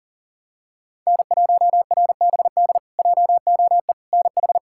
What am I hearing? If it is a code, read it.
Morse code, heard as a keyed tone, about 37 wpm, 700 Hz. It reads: N1RBD JOE NH
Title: K1USN SST QSO Practice - 40wpm